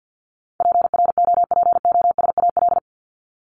Morse code, heard as a keyed tone, about 50 wpm, 700 Hz. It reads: PROPOSAL
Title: Top 1751-2000 Words 50wpm